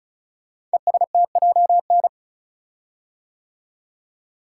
Morse code, heard as a keyed tone, about 35 wpm, 700 Hz. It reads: ESTJN
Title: Code Groups - 5 Letter:Numbers - 35wpm